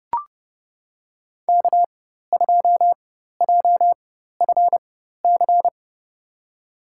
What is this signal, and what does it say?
Morse code, heard as a keyed tone, about 30 wpm, spaced as for 14 wpm, 700 Hz. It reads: K2JFC
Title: USA Call Signs - 5 Characters - 4x Character Spacing - 30wpm